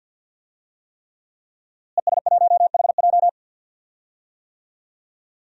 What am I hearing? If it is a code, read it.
Morse code, heard as a keyed tone, about 52 wpm, 700 Hz. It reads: ES1HJ
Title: DXCC-100 Call Signs - 5 Characters - Repeated 3x - 50wpm